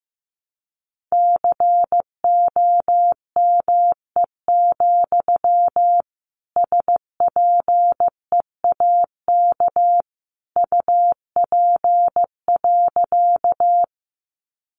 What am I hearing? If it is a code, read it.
Morse code, heard as a keyed tone, about 15 wpm, 700 Hz. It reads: COME, SPEAK UP.
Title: Sentences from Top 1100 Words 15wpm